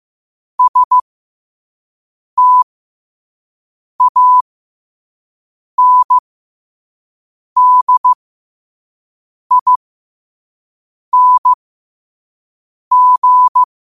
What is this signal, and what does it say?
Morse code, heard as a keyed tone, about 15 wpm, spaced as for 5 wpm, 1000 Hz. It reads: STANDING
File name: qotdfiles / 5wpm